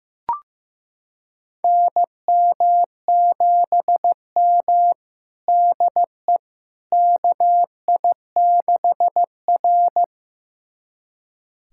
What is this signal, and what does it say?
Morse code, heard as a keyed tone, about 15 wpm, 700 Hz. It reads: NM7M DE KI6R
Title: USA Callsign DE Callsign - 4 Characters - 15wpm